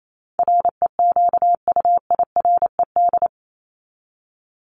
Morse code, heard as a keyed tone, about 28 wpm, 700 Hz. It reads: REQUIRED
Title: ED Suffix - Top 100 Words with Base Words 28wpm